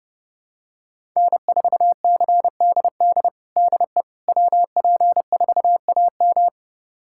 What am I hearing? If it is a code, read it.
Morse code, heard as a keyed tone, about 30 wpm, 700 Hz. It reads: N4CDD DE WP4AM
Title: USA Callsign DE Callsign - 5 Characters - 30wpm